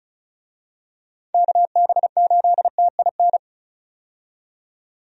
Morse code, heard as a keyed tone, about 34 wpm, 700 Hz. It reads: KB8TIN
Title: USA Call Signs - 6 Characters - 35wpm